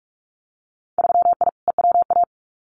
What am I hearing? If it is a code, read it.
Morse code, heard as a keyed tone, about 44 wpm, 700 Hz. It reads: 3I EWA
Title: Winter Field Day Exchanges Encore 45wpm